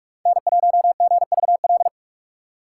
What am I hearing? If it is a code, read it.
Morse code, heard as a keyed tone, about 42 wpm, 700 Hz. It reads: N1GVL
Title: USA Call Signs - 5 Characters - 4x Character Spacing - 45wpm